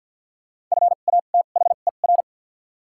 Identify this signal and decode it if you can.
Morse code, heard as a keyed tone, about 53 wpm, spaced as for 37 wpm, 700 Hz. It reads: FATHER
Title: 2x Character Spacing - Top 200 Words - 50wpm